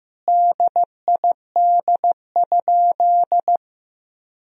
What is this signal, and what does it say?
Morse code, heard as a keyed tone, about 15 wpm, 700 Hz. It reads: DID?
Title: Sentences from Top 100 Words and Abbreviations 15wpm